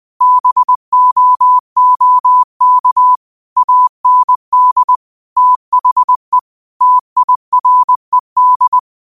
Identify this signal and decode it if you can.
Morse code, heard as a keyed tone, about 20 wpm, 1000 Hz. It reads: BOOK AND THE TIRED
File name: qotdfiles / 20wpm